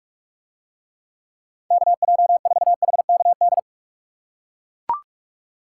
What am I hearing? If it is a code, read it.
Morse code, heard as a keyed tone, about 44 wpm, 700 Hz. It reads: KJ4HKD